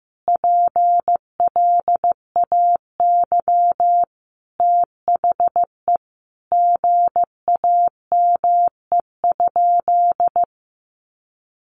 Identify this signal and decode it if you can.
Morse code, heard as a keyed tone, about 15 wpm, 700 Hz. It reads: PLAY THE GAME?
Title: Sentences from Top 1400 Words 15wpm